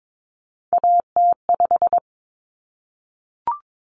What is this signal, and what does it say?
Morse code, heard as a keyed tone, about 22 wpm, 700 Hz. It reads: AT5
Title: Common Cut Numbers - 3-Digit 22wpm